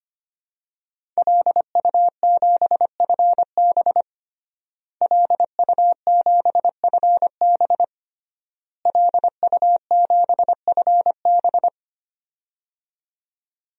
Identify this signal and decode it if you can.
Morse code, heard as a keyed tone, about 25 wpm, 700 Hz. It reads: LU7FB LU7FB LU7FB
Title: DXCC-100 Call Signs - 5 Characters - Repeated 3x - 25wpm